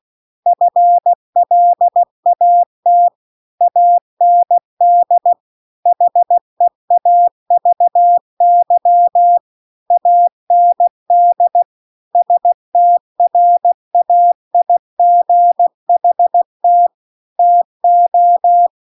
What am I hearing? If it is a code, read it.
Morse code, heard as a keyed tone, about 16 wpm, 700 Hz. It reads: FLAT AND HEAVY AND STRAIGHT TO